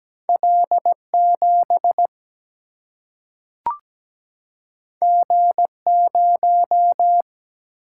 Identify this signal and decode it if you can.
Morse code, heard as a keyed tone, about 17 wpm, 700 Hz. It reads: L 7 G 0
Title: Single-Letter Single-Number 17wpm